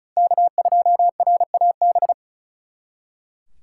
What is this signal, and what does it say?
Morse code, heard as a keyed tone, about 36 wpm, 700 Hz. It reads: K2RAB
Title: Speed Racing - USA 1x3 Call Signs 35wpm